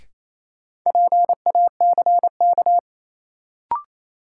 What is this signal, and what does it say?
Morse code, heard as a keyed tone, about 27 wpm, 700 Hz. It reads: PACK
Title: Sentences from Top 1300 Words - New Word Drill - 28wpm